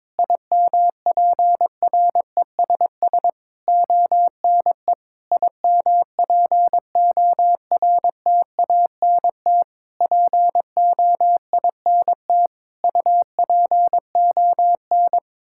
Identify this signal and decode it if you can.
Morse code, heard as a keyed tone, about 22 wpm, 700 Hz. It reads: IMPRESS ONE IMPORTANT POINT UPON
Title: Sentences from Top 1600 Words 22wpm